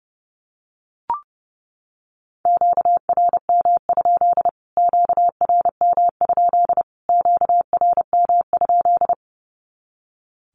Morse code, heard as a keyed tone, about 30 wpm, 700 Hz. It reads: QRM? QRM? QRM?
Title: Q-Codes Repeated 3x 30wpm